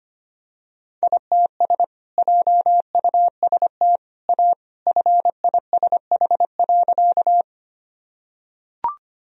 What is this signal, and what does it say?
Morse code, heard as a keyed tone, about 25 wpm, 700 Hz. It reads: ITS JUST A FISH.
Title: Sentences from Top 1000 Words - Review Part 1 - 25wpm